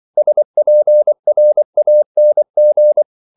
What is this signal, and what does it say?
Morse code, heard as a keyed tone, about 24 wpm, 600 Hz. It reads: SPRANG